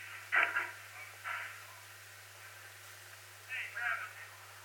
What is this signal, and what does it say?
Emergency Alert System (SAME) tones, heard as harsh buzzing bursts, with speech. It is heard from 1.1 to 4.2 s.